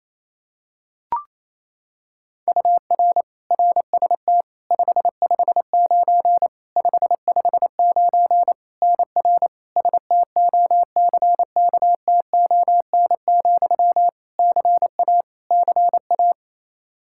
Morse code, heard as a keyed tone, about 28 wpm, 700 Hz. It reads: UR RST 559 559 NR STOCKTON, CA CA
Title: Practice QSO - Exchange 1 - Beginning - 28wpm